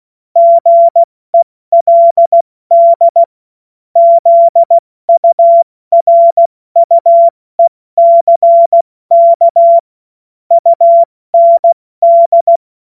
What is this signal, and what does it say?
Morse code, heard as a keyed tone, about 16 wpm, 680 Hz. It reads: GELD ZURUECK UND
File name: Morsekurs / Witze / Witz 05 Dezember 2023 70 F80